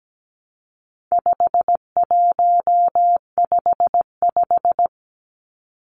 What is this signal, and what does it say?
Morse code, heard as a keyed tone, about 17 wpm, 700 Hz. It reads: 5155
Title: Numbers - 4-Digit 17wpm